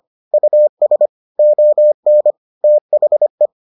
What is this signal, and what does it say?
Morse code, heard as a keyed tone, about 25 wpm, 600 Hz. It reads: US ON THE